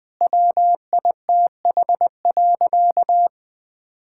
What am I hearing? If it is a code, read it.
Morse code, heard as a keyed tone, about 20 wpm, 700 Hz. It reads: WITH.
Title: All-Age Appropriate Jokes - 20wpm